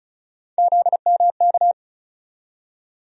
Morse code, heard as a keyed tone, about 36 wpm, 700 Hz. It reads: ZMK
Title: USA Call Sign Postfixes - 3 Letters 35wpm